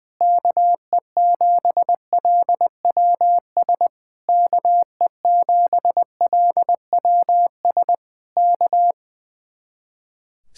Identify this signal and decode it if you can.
Morse code, heard as a keyed tone, about 20 wpm, 700 Hz. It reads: KE7LWS KE7LWS K
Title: Speed-Racing Common QSO Phrases - 20wpm